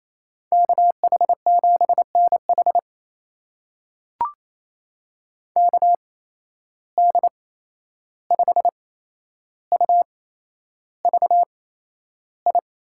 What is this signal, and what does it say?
Morse code, heard as a keyed tone, about 28 wpm, spaced as for 11 wpm, 700 Hz. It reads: KH7NH KD5UVI
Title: USA Call Signs - 8x Character Spacing - 28wpm